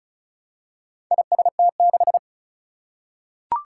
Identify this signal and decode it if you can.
Morse code, heard as a keyed tone, about 36 wpm, 700 Hz. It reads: IST6
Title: Four Challenging Letter-Number Combinations 35wpm